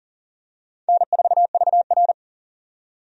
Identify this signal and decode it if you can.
Morse code, heard as a keyed tone, about 39 wpm, 700 Hz. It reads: N4VR
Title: USA Call Signs - 4 Characters - 40wpm